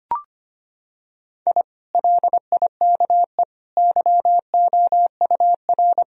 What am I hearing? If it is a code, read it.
Morse code, heard as a keyed tone, about 25 wpm, 700 Hz. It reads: I LIKE YOUR
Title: Sentences from Top 1000 Words - Review Part 4 - 25wpm